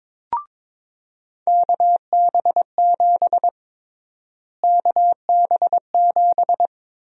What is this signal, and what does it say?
Morse code, heard as a keyed tone, about 22 wpm, 700 Hz. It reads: KB7 KB7
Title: USA Call Sign Prefixes - Repeated 2x - 22wpm